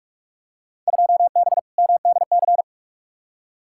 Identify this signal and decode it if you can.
Morse code, heard as a keyed tone, about 47 wpm, 700 Hz. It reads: 2B MDC
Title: Field Day Exchanges - 45wpm